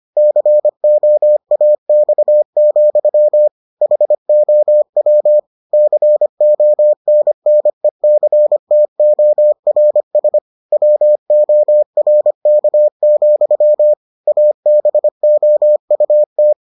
Morse code, heard as a keyed tone, about 25 wpm, 600 Hz. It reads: COAX, HOW CONNECTORS WORK, ABOUT